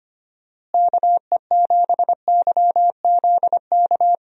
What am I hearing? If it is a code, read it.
Morse code, heard as a keyed tone, about 25 wpm, 700 Hz. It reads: KE7YZK
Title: USA Call Signs - 6 Characters - Repeated 2x - 25wpm